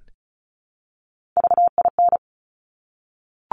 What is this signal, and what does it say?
Morse code, heard as a keyed tone, about 36 wpm, 700 Hz. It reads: VIN
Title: Top Three Letter Combinations in Words 35wpm